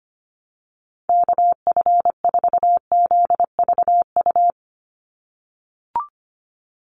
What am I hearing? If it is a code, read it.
Morse code, heard as a keyed tone, about 25 wpm, 700 Hz. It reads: KF4ZVU
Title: USA Call Signs 25wpm